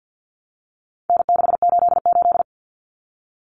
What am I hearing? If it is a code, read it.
Morse code, heard as a keyed tone, about 50 wpm, 700 Hz. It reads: N688
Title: Common Cut Numbers - 4-Digit 50wpm